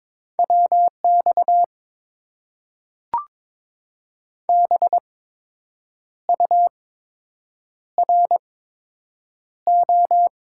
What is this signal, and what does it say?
Morse code, heard as a keyed tone, about 22 wpm, spaced as for 7 wpm, 700 Hz. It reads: WX BURO
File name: Top 90 QSO Elements - 8x Character Spacing - 22wpm